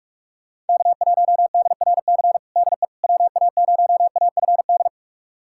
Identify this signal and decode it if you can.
Morse code, heard as a keyed tone, about 41 wpm, 700 Hz. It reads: K1DRC DE WA0AFD
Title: USA Callsign DE Callsign 45wpm